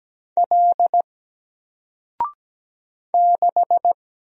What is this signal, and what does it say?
Morse code, heard as a keyed tone, about 17 wpm, 700 Hz. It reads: L 6
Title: Single Letter-Number Alt 17wpm